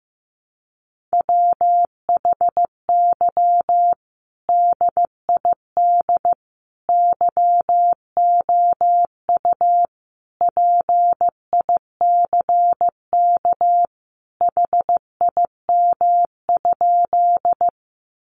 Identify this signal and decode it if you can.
Morse code, heard as a keyed tone, about 15 wpm, 700 Hz. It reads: WHY DID YOU PICK HIM?
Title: Sentences from Top 1000 Words 15wpm